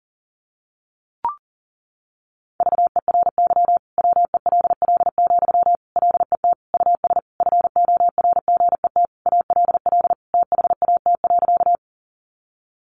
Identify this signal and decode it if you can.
Morse code, heard as a keyed tone, about 37 wpm, 700 Hz. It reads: VERY WELL, LET US FORGET ALL THAT.
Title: Sentences from Top 1200 Words 40wpm